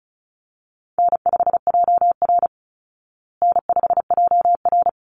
Morse code, heard as a keyed tone, about 34 wpm, 700 Hz. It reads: N5JR N5JR
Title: Easy Call Signs Repeated 3x - 35wpm